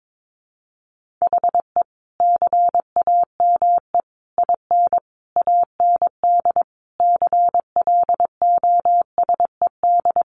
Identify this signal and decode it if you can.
Morse code, heard as a keyed tone, about 22 wpm, 700 Hz. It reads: HE CAME IN AND CLOSED
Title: Sentences from Top 1400 Words 22wpm